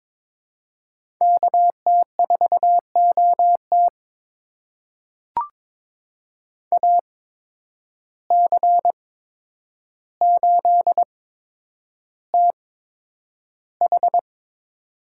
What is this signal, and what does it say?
Morse code, heard as a keyed tone, about 22 wpm, spaced as for 9 wpm, 700 Hz. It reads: KT4OT AC8TH
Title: USA Call Signs - 5 Characters - 8x Character Spacing - 22wpm